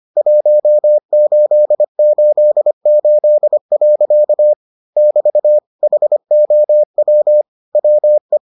Morse code, heard as a keyed tone, about 25 wpm, 600 Hz. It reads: 1888. <BT> HOW WE